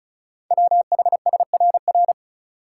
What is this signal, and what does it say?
Morse code, heard as a keyed tone, about 36 wpm, 690 Hz. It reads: WHSRR
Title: Code Groups - 5 Letters - 35wpm